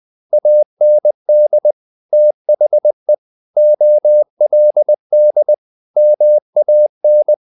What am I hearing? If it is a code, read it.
Morse code, heard as a keyed tone, about 20 wpm, 600 Hz. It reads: AND THE OLD MAN
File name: the old man and the sea / the old man and the sea part 1 20wpm